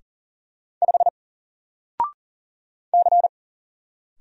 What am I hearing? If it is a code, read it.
Morse code, heard as a keyed tone, about 40 wpm, 700 Hz. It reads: 5 C
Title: Single Letter-Number Alt 40wpm